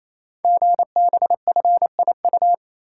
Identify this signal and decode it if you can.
Morse code, heard as a keyed tone, about 29 wpm, 700 Hz. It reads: GBFIU
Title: Code Groups - 5 Letters - 28wpm